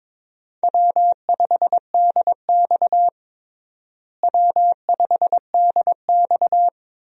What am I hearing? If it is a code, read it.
Morse code, heard as a keyed tone, about 22 wpm, 700 Hz. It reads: W5DX W5DX
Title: USA Call Signs - 4 Characters - Repeated 2x - 22wpm